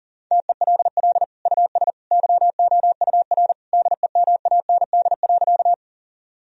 Morse code, heard as a keyed tone, about 40 wpm, 700 Hz. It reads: TELL US YOUR DEMAND.
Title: Sentences from Top 1000 Words - Review Part 4 - 40wpm